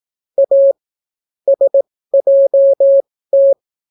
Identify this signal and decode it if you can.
Morse code, heard as a keyed tone, about 18 wpm, spaced as for 14 wpm, 550 Hz. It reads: A SJT